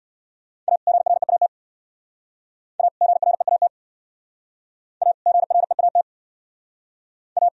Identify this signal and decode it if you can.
Morse code, heard as a keyed tone, about 62 wpm, 690 Hz. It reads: A GREAT A GREAT A GREAT A
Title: Speed-Racing - Sets of 2 Words 50wpm